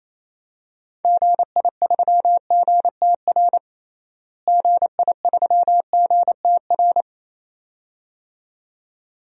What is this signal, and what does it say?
Morse code, heard as a keyed tone, about 28 wpm, 700 Hz. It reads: GI3GTR GI3GTR
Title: DXCC-100 Call Signs - 6 Characters - Repeated 2x - 28wpm